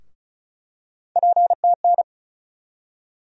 Morse code, heard as a keyed tone, about 34 wpm, 700 Hz. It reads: PTN
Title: USA Call Sign Postfixes - 3 Letters 35wpm